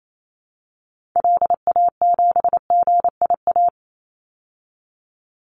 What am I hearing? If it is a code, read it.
Morse code, heard as a keyed tone, about 28 wpm, 700 Hz. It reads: LA7GIA
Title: Call Signs Repeated 2x - 28wpm